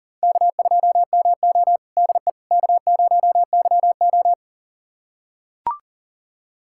Morse code, heard as a keyed tone, about 40 wpm, 700 Hz. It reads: K2MO DE K0YO
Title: USA Callsign DE Callsign - 4 Characters - 40wpm